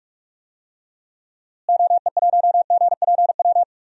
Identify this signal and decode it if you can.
Morse code, heard as a keyed tone, about 46 wpm, 700 Hz. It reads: OE1GPW
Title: DXCC-100 Call Signs - 6 Characters - Repeated 3x - 45wpm